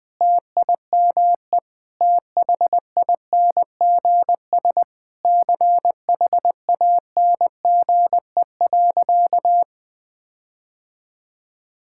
Morse code, heard as a keyed tone, about 20 wpm, 700 Hz. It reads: TIME THINGS CHANGE.
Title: Sentences with 6-Word Phrases - 20wpm